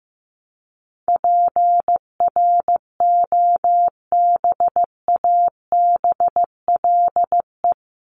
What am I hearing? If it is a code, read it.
Morse code, heard as a keyed tone, about 15 wpm, 700 Hz. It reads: PROBABLE
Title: LY Suffix - Top 100 Words with Base Words 15wpm